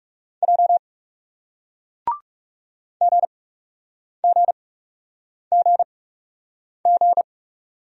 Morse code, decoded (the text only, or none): J GGGG